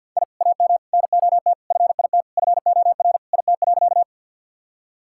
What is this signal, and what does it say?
Morse code, heard as a keyed tone, about 50 wpm, 700 Hz. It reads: I AM NOT FIT FOR IT.
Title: Sentences from Top 1000 Words - Review Part 5 - 50wpm